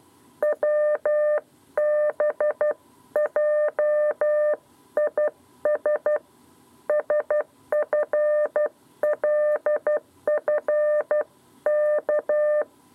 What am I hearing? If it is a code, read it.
Morse code, heard as a keyed tone, about 12 wpm, 600 Hz. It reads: WBJIS SFLFK
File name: 7b lezione cw 2